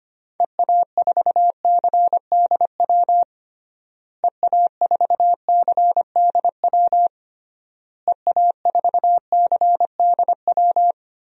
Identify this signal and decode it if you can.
Morse code, heard as a keyed tone, about 25 wpm, 700 Hz. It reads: EA4CDW EA4CDW EA4CDW
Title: DXCC-100 Call Signs - 6 Characters - Repeated 3x - 25wpm